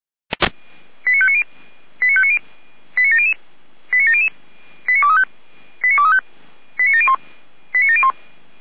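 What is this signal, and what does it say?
Five-tone selective call (ZVEI), heard as quick runs of five tones, tones of about 80 ms. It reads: 89590 89590 89700 89700 89235 89235 89701 89701
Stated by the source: ZVEI five-tone call 89120 received by SDR.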